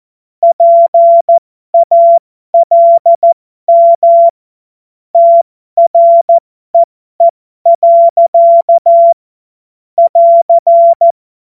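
Morse code, heard as a keyed tone, about 14 wpm, spaced as for 12 wpm, 680 Hz. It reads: PALM TREE. <AR>
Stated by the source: Amateur radio CW practice text made of jokes.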